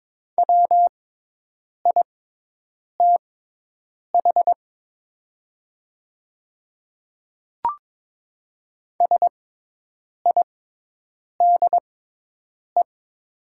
Morse code, heard as a keyed tone, about 22 wpm, spaced as for 8 wpm, 700 Hz. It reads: WITH SIDE
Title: 6x Character Spacing - Top 100 Words - 22wpm